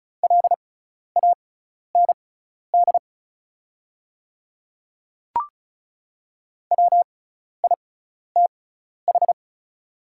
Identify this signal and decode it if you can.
Morse code, heard as a keyed tone, about 34 wpm, spaced as for 12 wpm, 700 Hz. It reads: LAND WITH
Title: 6x Character Spacing - Top 500 Words - 35wpm